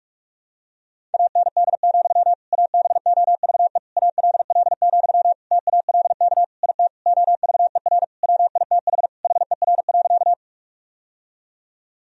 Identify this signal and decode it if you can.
Morse code, heard as a keyed tone, about 46 wpm, 700 Hz. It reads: AND, ABOVE ALL, TALK IT OVER WITH HER.
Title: Sentences from Top 1000 Words - Review Part 2 - 45wpm